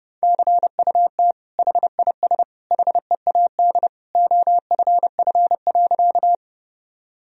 Morse code, heard as a keyed tone, about 30 wpm, 700 Hz. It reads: CUT HIS HEAD OFF.
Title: Sentences from Top 1000 Words - Review Part 1 - 30wpm